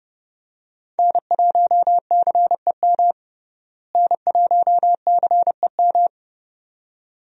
Morse code, heard as a keyed tone, about 32 wpm, 700 Hz. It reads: N1CEM N1CEM
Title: USA Call Signs - 5 Characters - Repeated 2x - 30wpm